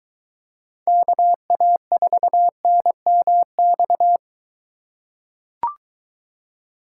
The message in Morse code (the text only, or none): KA4NMX